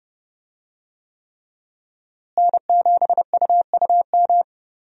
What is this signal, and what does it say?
Morse code, heard as a keyed tone, about 30 wpm, 700 Hz. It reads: N7UUM
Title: USA Call Signs - 5 Characters - Repeated 3x - 30wpm